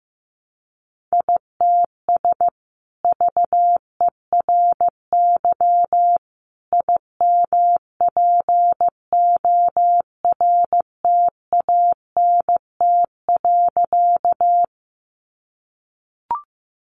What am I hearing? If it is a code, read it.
Morse code, heard as a keyed tone, about 15 wpm, 700 Hz. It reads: ITS VERY IMPORTANT.
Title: Sentences from Top 600 Words 15wpm